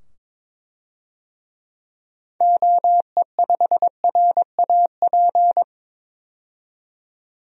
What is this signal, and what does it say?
Morse code, heard as a keyed tone, about 22 wpm, 700 Hz. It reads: OE5RAP
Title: DXCC-100 Call Signs - 6 Characters - 22wpm